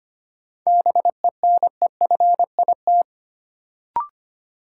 Morse code, heard as a keyed tone, about 25 wpm, 700 Hz. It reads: BENEFIT
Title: Sentences from Top 1000 Words - Review Part 6 - New Word Drill - 25wpm